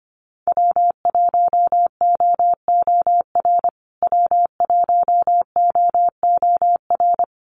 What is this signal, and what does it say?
Morse code, heard as a keyed tone, about 25 wpm, 700 Hz. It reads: W1OOR W1OOR
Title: USA Call Signs Repeated 2x 25wpm